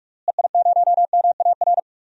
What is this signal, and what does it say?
Morse code, heard as a keyed tone, about 46 wpm, 700 Hz. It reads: EI0MAR
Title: DXCC-100 Call Signs - 6 Characters - Repeated 3x - 45wpm